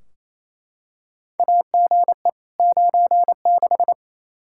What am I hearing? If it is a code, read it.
Morse code, heard as a keyed tone, about 28 wpm, 700 Hz. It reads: AGE 96
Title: Practice QSO - Exchange 3 - More Info - 28wpm